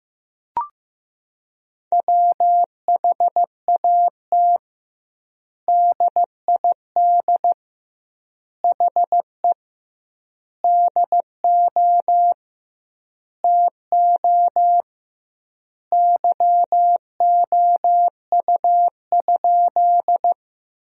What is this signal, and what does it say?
Morse code, heard as a keyed tone, about 15 wpm, 700 Hz. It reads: WHAT DID HE DO TO YOU?
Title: 2x Word-Spacing - Sentences from Top 100 Words - 15wpm